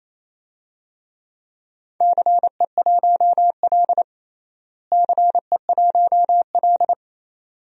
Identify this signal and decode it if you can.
Morse code, heard as a keyed tone, about 28 wpm, 700 Hz. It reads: CE1L CE1L
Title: DXCC-100 Call Signs - 4 Characters - Repeated 2x - 28wpm